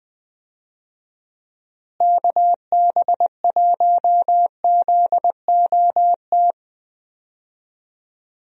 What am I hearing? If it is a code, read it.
Morse code, heard as a keyed tone, about 21 wpm, 700 Hz. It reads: KB1ZOT
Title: USA Call Signs - 6 Characters - Repeated 3x - 20wpm